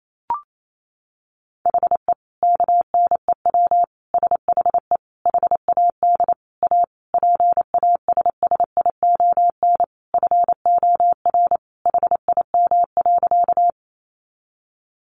Morse code, heard as a keyed tone, about 28 wpm, 700 Hz. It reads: HE KNEW SHE HAD A PASSION FOR HIM.
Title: Sentences from Top 1300 Words 28wpm